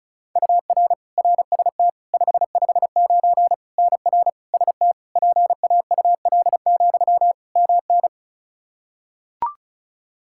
Morse code, heard as a keyed tone, about 36 wpm, 700 Hz. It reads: UR RST 559 NR ST PAUL, MN
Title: Practice QSO - Exchange 1 - Beginning - 35wpm